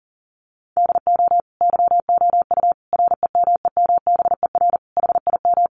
Morse code, heard as a keyed tone, about 40 wpm, 700 Hz. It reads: DO YOU REMEMBER HIM
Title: Sentences from Top 1300 Words 40wpm